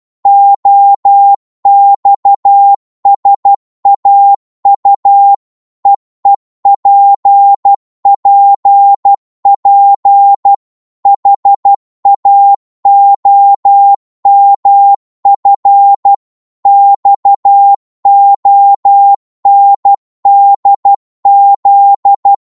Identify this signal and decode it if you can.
Morse code, heard as a keyed tone, about 12 wpm, 800 Hz. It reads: OXSAU EEPPP HAOMF XONDZ
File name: Morse / ltr-12wpm